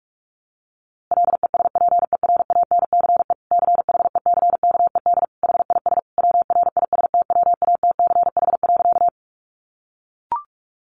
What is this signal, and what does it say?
Morse code, heard as a keyed tone, about 44 wpm, 700 Hz. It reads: LESPERANCE CHECKED HIS WRISTWATCH.
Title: A Sound Of Thunder 45wpm